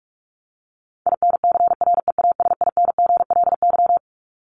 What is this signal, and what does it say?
Morse code, heard as a keyed tone, about 44 wpm, 700 Hz. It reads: INCREASINGLY
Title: LY Suffix - Top 100 Words with Base Words 45wpm